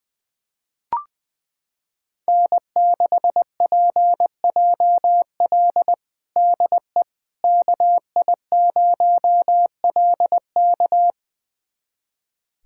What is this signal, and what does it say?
Morse code, heard as a keyed tone, about 21 wpm, 700 Hz. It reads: N6PJL DE KI0LK
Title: USA Callsign DE Callsign - 5 Characters - 20wpm